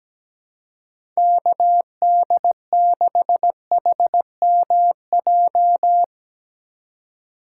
Morse code, heard as a keyed tone, about 17 wpm, 700 Hz. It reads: KD6HMJ
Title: USA Call Signs - 6 Characters - 3x Character Spacing - 17wpm